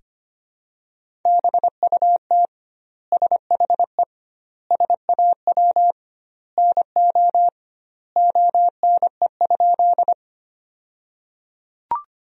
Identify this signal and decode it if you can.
Morse code, heard as a keyed tone, about 25 wpm, 700 Hz. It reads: BUT SHE SAW NO ONE?
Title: 2x Word-Spacing - Sentences from Top 300 Words - 25wpm